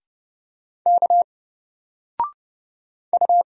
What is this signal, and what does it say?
Morse code, heard as a keyed tone, about 30 wpm, 700 Hz. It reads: K U
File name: Single-Letter-Number-30wpm